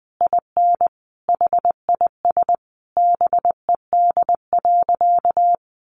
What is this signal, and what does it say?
Morse code, heard as a keyed tone, about 20 wpm, 700 Hz. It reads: IN HIS BED.
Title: Sentences from Top 500 Words 20wpm